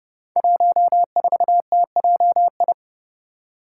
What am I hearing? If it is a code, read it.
Morse code, heard as a keyed tone, about 32 wpm, 700 Hz. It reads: 14TJI
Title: Five Challenging Letter-Number Combinations 30wpm